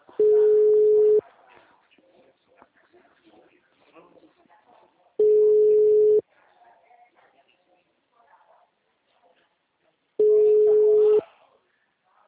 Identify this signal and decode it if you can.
Ringback tone (Europe, 425 Hz), with speech.
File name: out- -2015- - - .6826